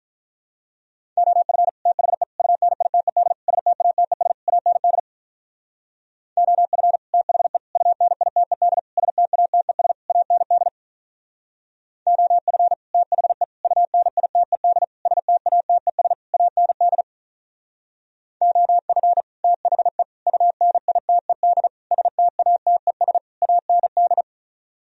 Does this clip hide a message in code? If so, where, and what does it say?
Yes: Morse code from 1.2 to 24.2 s, heard as a keyed tone, about 44 wpm, 700 Hz. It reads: OF THE UNITED STATES AND OF THE UNITED STATES AND OF THE UNITED STATES AND OF THE UNITED STATES AND